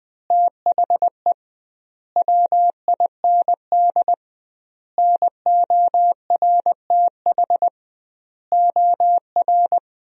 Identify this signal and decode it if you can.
Morse code, heard as a keyed tone, about 20 wpm, 700 Hz. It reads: THE WIND NORTH OR
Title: 2x Word-Spacing - Sentences from Top 400 Words - 20wpm